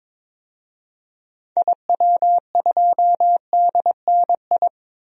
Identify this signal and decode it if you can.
Morse code, heard as a keyed tone, about 21 wpm, 700 Hz. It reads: IW2DNI
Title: DXCC-100 Call Signs - 6 Characters - 22wpm